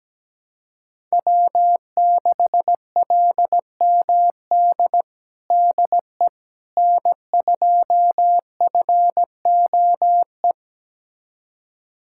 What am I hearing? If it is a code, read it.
Morse code, heard as a keyed tone, about 17 wpm, 700 Hz. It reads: W6LMD DE N2FOE
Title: USA Callsign DE Callsign - 5 Characters - 17wpm